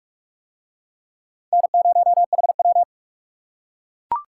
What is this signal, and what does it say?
Morse code, heard as a keyed tone, about 41 wpm, 700 Hz. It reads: N0HW